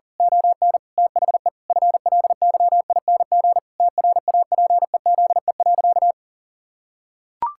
Morse code, heard as a keyed tone, about 37 wpm, 700 Hz. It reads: ON THE FLYING TRAPEZE.
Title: Pangram Challenge v2 - 40wpm